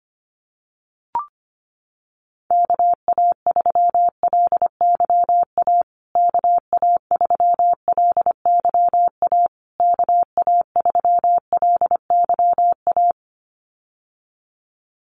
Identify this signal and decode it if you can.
Morse code, heard as a keyed tone, about 25 wpm, 700 Hz. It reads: KA3LYA KA3LYA KA3LYA
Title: USA Call Signs Repeated 3x 25wpm